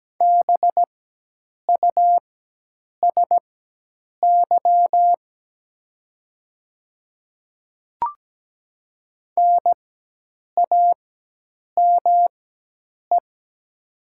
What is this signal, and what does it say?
Morse code, heard as a keyed tone, about 17 wpm, spaced as for 8 wpm, 700 Hz. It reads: BUSY NAME
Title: 4x Character Spacing - Top 500 Words - 17wpm